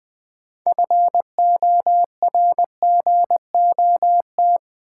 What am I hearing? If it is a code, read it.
Morse code, heard as a keyed tone, about 20 wpm, 700 Hz. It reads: FORGOT
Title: Sentences from Top 1400 Words - New Word Drill Alt - 20wpm